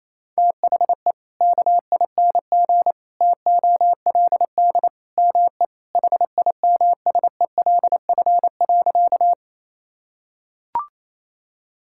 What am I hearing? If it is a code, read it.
Morse code, heard as a keyed tone, about 28 wpm, 700 Hz. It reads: THE KING TOLD ME HIMSELF.
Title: Sentences from Top 1600 Words 28wpm